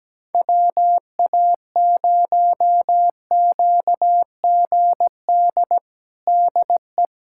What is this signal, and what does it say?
Morse code, heard as a keyed tone, about 17 wpm, 700 Hz. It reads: WA0QGD DE
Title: USA Callsign DE Callsign 17wpm